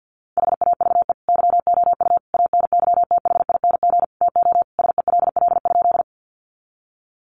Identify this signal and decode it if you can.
Morse code, heard as a keyed tone, about 50 wpm, 700 Hz. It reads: HAVE YOU ANYTHING TO SELL?
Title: Sentences from Top 1100 Words 50wpm